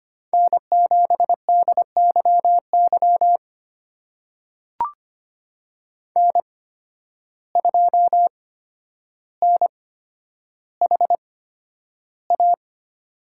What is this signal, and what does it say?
Morse code, heard as a keyed tone, about 25 wpm, spaced as for 11 wpm, 700 Hz. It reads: N7DYY N2NHA
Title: USA Call Signs - 5 Characters - 8x Character Spacing - 25wpm